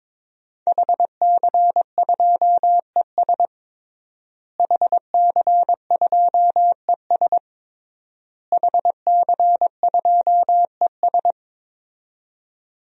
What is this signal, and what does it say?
Morse code, heard as a keyed tone, about 22 wpm, 700 Hz. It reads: HC2ES HC2ES HC2ES
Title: DXCC-100 Call Signs - 5 Characters - Repeated 3x - 22wpm